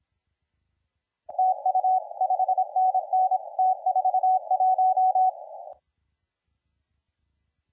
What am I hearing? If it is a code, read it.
Morse code, heard as a keyed tone, about 25 wpm, 750 Hz, with radio noise. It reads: TU 5NN T41
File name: la4c / sac-cw-2017 / CONTEST AUDIO 1100Z